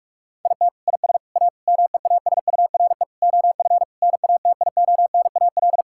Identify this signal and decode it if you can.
Morse code, heard as a keyed tone, about 46 wpm, 700 Hz. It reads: IT IS A MEASURE OF NATIONAL
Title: Sentences from Top 1600 Words 45wpm